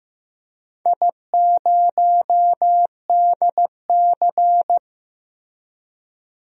Morse code, heard as a keyed tone, about 15 wpm, 700 Hz. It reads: I0DC
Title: DXCC-100 Call Signs - 4 Characters - 15wpm